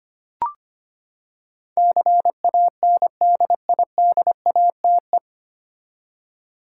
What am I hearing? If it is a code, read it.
Morse code, heard as a keyed tone, about 25 wpm, 700 Hz. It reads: CANDIDATE
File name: Nine Letter Words - 25wpm